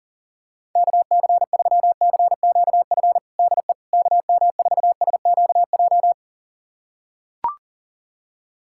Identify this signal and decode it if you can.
Morse code, heard as a keyed tone, about 40 wpm, 700 Hz. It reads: KC3CQF DE KM4SQJ
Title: USA Callsign DE Callsign - 6 Characters - 40wpm